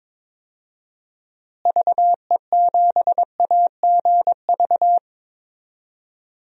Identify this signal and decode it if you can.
Morse code, heard as a keyed tone, about 22 wpm, 700 Hz. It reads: VE7AGV
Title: DXCC-100 Call Signs - 6 Characters - Repeated 3x - 22wpm